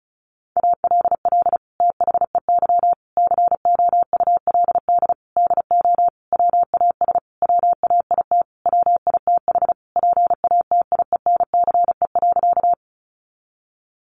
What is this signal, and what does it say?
Morse code, heard as a keyed tone, about 34 wpm, 700 Hz. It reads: ALL THEY COULD DO WAS WAIT WITH PATIENCE.
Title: Sentences from Top 1300 Words 35wpm